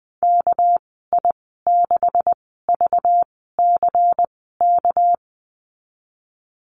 Morse code, heard as a keyed tone, about 19 wpm, 700 Hz. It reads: K I 6 V C K
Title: 200 Lesson 32g - Letter k - Call Signs 20wpm